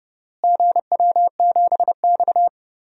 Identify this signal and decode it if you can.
Morse code, heard as a keyed tone, about 31 wpm, 700 Hz. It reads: GW7X
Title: DXCC-100 Call Signs - 4 Characters - Repeated 3x - 30wpm